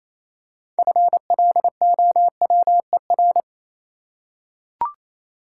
Morse code, heard as a keyed tone, about 27 wpm, 700 Hz. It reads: FLOWER
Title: Six Letter Words - 28wpm